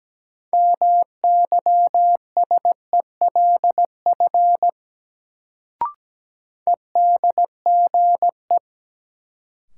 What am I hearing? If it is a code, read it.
Morse code, heard as a keyed tone, about 17 wpm, 700 Hz. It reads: MYSELF EDGE
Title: Sentences from Top 1300 Words - New Word Drill Alt - 17wpm